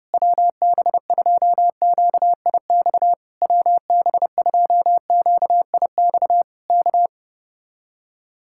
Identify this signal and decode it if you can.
Morse code, heard as a keyed tone, about 30 wpm, 700 Hz. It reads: WB2QIX WB2QIX K